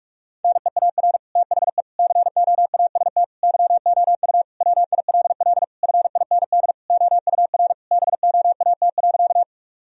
Morse code, heard as a keyed tone, about 46 wpm, 700 Hz. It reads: NEAR THE COAST YOU WILL FIND OUR BOAT.
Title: Sentences from Top 1000 Words - Review Part 4 - 45wpm